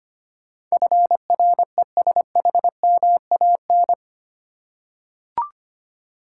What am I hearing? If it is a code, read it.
Morse code, heard as a keyed tone, about 25 wpm, 700 Hz. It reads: FRESHMAN
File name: 8 Letter Words - First 3 Letters Given - 25wpm